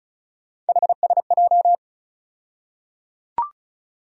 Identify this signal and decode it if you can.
Morse code, heard as a keyed tone, about 34 wpm, 700 Hz. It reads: HSJ
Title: USA Call Sign Postfixes - 3 Letters 35wpm